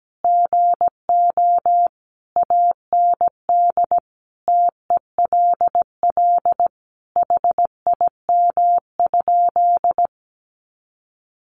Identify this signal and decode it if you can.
Morse code, heard as a keyed tone, about 17 wpm, 700 Hz. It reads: GO AND TELL HIM?
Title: Sentences from Top 200 Words 17wpm